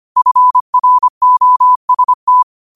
Morse code, heard as a keyed tone, about 25 wpm, 1000 Hz. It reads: FROST
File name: qotdfiles / 25wpm